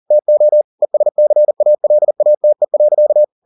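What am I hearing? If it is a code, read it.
Morse code, heard as a keyed tone, about 40 wpm, 600 Hz. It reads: TO ESCALATE.
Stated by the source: Morse code at 40 wpm, practice text, CW med Gnister.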